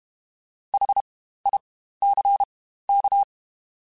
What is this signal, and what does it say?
Morse code, heard as a keyed tone, about 32 wpm, spaced as for 15 wpm, 800 Hz. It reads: HICK
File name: Spaced Alphas / Spaced Alphas mp3s / Hs sp @30